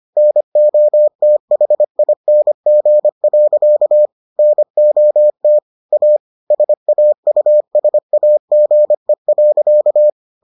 Morse code, heard as a keyed tone, about 25 wpm, 600 Hz. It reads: NOTHING. NOT A SAUSAGE.